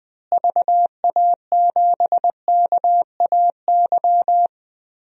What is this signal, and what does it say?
Morse code, heard as a keyed tone, about 21 wpm, 700 Hz. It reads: VA7KAY
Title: DXCC-100 Call Signs - 6 Characters - Repeated 2x - 20wpm